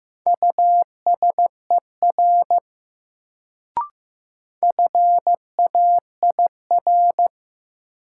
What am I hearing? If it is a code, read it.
Morse code, heard as a keyed tone, about 16 wpm, 700 Hz. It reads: USER FAIR
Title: Sentences from Top 1000 Words - Review Part 5 - New Word Drill - 15wpm